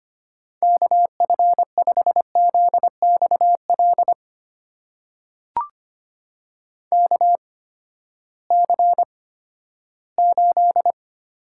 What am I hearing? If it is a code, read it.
Morse code, heard as a keyed tone, about 25 wpm, spaced as for 14 wpm, 700 Hz. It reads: KF5ZXL KC8
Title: USA Call Signs - 6 Characters - 8x Character Spacing - 25wpm